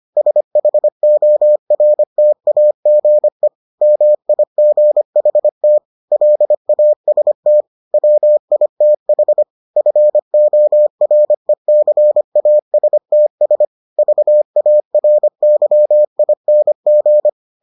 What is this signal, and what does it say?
Morse code, heard as a keyed tone, about 25 wpm, 600 Hz. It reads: SHORTAGE MIGHT LAST WITH FORECASTS VARYING